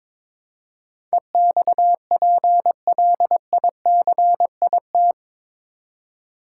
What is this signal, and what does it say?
Morse code, heard as a keyed tone, about 22 wpm, 700 Hz. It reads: EXPLICIT
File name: practice / 22wpm / Eight Letter Words - 22wpm